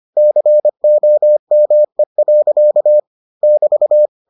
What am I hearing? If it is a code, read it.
Morse code, heard as a keyed tone, about 25 wpm, 600 Hz. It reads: COME. <BT>